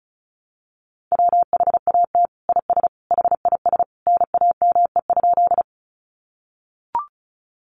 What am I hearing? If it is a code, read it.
Morse code, heard as a keyed tone, about 34 wpm, 700 Hz. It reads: WHAT IS HIS NAME?
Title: Sentences from Top 200 Words 35wpm